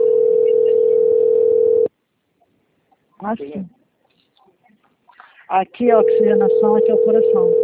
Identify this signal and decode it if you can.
Ringback tone (North America), 440 + 480 Hz, with speech.
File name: out- - - - - .52549